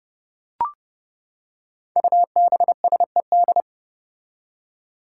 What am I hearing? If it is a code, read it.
Morse code, heard as a keyed tone, about 30 wpm, 700 Hz. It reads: UBSED
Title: Five Challenging Letter-Number Combinations 30wpm